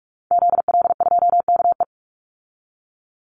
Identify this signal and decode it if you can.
Morse code, heard as a keyed tone, about 44 wpm, 700 Hz. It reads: ZL2KE
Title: Call Signs Repeated 2x - 45wpm